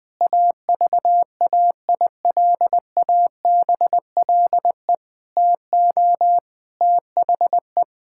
Morse code, heard as a keyed tone, about 21 wpm, 700 Hz. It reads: AVAILABLE TO THE
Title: Sentences from Top 1000 Words - Review Part 5 - 20wpm